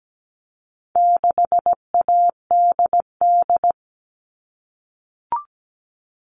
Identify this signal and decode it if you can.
Morse code, heard as a keyed tone, about 17 wpm, 700 Hz. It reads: 6ADD
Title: All Cut Numbers - 4-Digit 17wpm